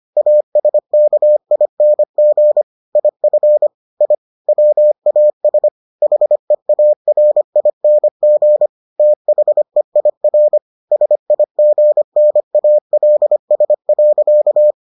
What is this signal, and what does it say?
Morse code, heard as a keyed tone, about 25 wpm, 600 Hz. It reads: ASKING IF I WAS HEARING THEIR SIGNALS.